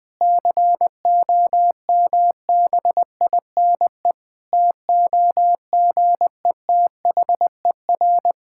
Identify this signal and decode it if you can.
Morse code, heard as a keyed tone, about 20 wpm, 700 Hz. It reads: COMBINE TOGETHER
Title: Sentences from Top 1600 Words 20wpm